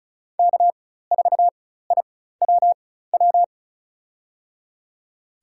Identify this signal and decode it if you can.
Morse code, heard as a keyed tone, about 36 wpm, spaced as for 16 wpm, 700 Hz. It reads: K4IWW
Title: USA Call Signs - 5 Characters - 4x Character Spacing - 35wpm